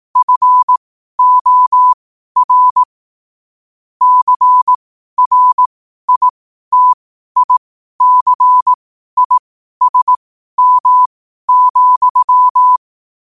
Morse code, heard as a keyed tone, about 18 wpm, spaced as for 12 wpm, 1000 Hz. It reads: FOR CRITICISM,
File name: qotdfiles / 12wpm